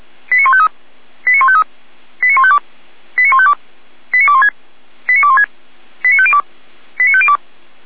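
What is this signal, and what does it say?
Five-tone selective call (ZVEI), heard as quick runs of five tones, tones of about 70 ms. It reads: 89153 89153 89152 89152 89216 89216 89502 89502
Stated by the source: ZVEI five-tone selective call 89060, received on an SDR.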